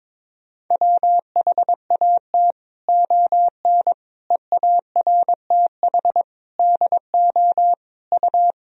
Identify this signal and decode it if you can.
Morse code, heard as a keyed tone, about 22 wpm, 700 Hz. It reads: WHAT ON EARTH DO U
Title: Sentences from Top 200 Words and Abbreviations 22wpm